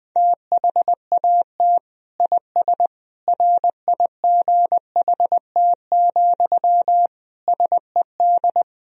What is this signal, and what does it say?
Morse code, heard as a keyed tone, about 20 wpm, 700 Hz. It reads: THAT IS RIGHT, SED
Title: Sentences from Top 200 Words and Abbreviations 20wpm